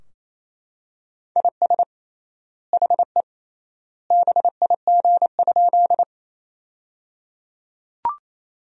Morse code, heard as a keyed tone, about 28 wpm, 700 Hz. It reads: IS HE BIG?
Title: 3x Word-Spacing - Sentences from Top 200 Words - 28wpm